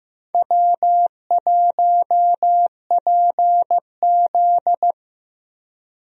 Morse code, heard as a keyed tone, about 15 wpm, 700 Hz. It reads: W1PZ
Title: USA Call Signs - 4 Characters - Repeated 2x - 15wpm